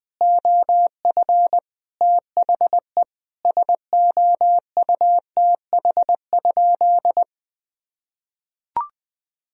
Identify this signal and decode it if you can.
Morse code, heard as a keyed tone, about 20 wpm, 700 Hz. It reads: OF THE SOUTH?
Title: Sentences from Top 1000 Words - Review Part 2 - 20wpm